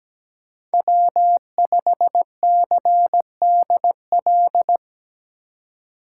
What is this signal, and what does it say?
Morse code, heard as a keyed tone, about 17 wpm, 700 Hz. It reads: W5CDL
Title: USA Call Signs - 5 Characters - 3x Character Spacing - 17wpm